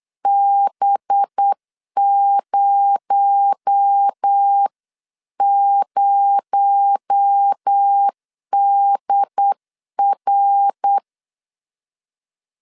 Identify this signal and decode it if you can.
Morse code, heard as a keyed tone, about 8 wpm, 800 Hz. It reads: B00DR
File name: CW / DET / 1 6 1500C S B 8wpm 5142